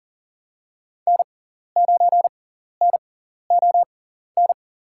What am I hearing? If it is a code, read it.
Morse code, heard as a keyed tone, about 40 wpm, spaced as for 14 wpm, 700 Hz. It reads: N9NON
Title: USA Call Signs - 5 Characters - 6x Character Spacing - 40wpm